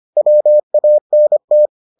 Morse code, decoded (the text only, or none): WANT